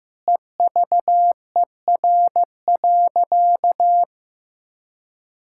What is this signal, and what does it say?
Morse code, heard as a keyed tone, about 16 wpm, 700 Hz. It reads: EVER.
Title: Sentences from Top 1600 Words 15wpm